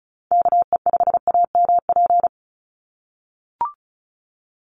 Morse code, heard as a keyed tone, about 34 wpm, 700 Hz. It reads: KE5AMP
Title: USA Call Signs 35wpm